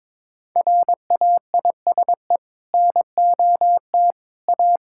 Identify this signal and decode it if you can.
Morse code, heard as a keyed tone, about 22 wpm, 700 Hz. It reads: RAISE NOT A
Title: Sentences from Top 1600 Words 22wpm